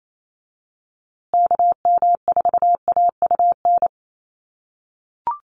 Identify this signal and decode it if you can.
Morse code, heard as a keyed tone, about 28 wpm, 700 Hz. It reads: KM4AUN
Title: USA Call Signs Repeated 2x 28wpm